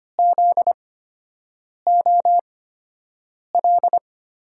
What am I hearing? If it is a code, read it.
Morse code, heard as a keyed tone, about 25 wpm, spaced as for 7 wpm, 700 Hz. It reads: ZOL
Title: USA Call Signs - 5 Characters - 8x Character Spacing - 25wpm